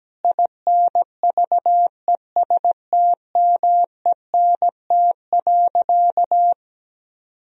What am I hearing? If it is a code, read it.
Morse code, heard as a keyed tone, about 17 wpm, 700 Hz. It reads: INVESTMENT.
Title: Sentences from Top 1000 Words - Review Part 4 - 17wpm